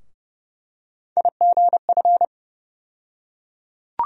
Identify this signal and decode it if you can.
Morse code, heard as a keyed tone, about 30 wpm, 700 Hz. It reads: IGF
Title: USA Call Sign Postfixes - 3 Letters 30wpm